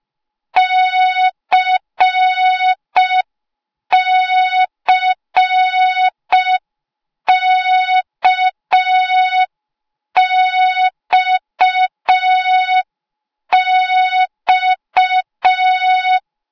Morse code, heard as a keyed tone, about 5 wpm, 3700 Hz. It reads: CCKXX
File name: Morze-HA2MN / hang / 08KXC HA2MN